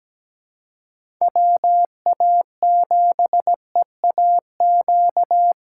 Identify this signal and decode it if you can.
Morse code, heard as a keyed tone, about 17 wpm, 700 Hz. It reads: WA7EAQ
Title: USA Callsign DE Callsign 17wpm